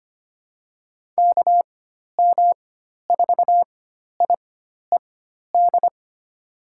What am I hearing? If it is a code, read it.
Morse code, heard as a keyed tone, about 25 wpm, spaced as for 12 wpm, 700 Hz. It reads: KM4IED
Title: USA Call Signs - 6 Characters - 4x Character Spacing - 25wpm